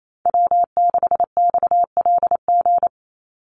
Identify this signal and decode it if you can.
Morse code, heard as a keyed tone, about 28 wpm, 700 Hz. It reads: W6XLG
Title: USA Call Signs 28wpm